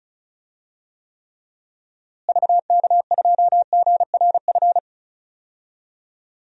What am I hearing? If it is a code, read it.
Morse code, heard as a keyed tone, about 36 wpm, 700 Hz. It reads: VK2GRF
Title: DXCC-100 Call Signs - 6 Characters - Repeated 3x - 35wpm